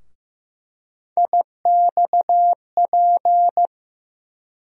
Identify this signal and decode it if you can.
Morse code, heard as a keyed tone, about 15 wpm, 700 Hz. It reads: IXP
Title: USA Call Sign Postfixes - 3 Letters 15wpm